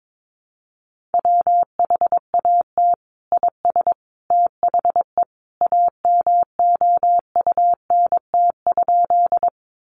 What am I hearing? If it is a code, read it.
Morse code, heard as a keyed tone, about 22 wpm, 700 Hz. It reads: WHAT IS THE AMOUNT?
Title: Sentences from Top 700 Words 22wpm